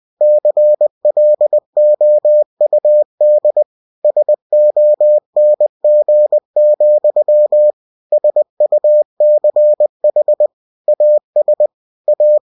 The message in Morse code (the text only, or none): CLOUD SONG, SUCH AS A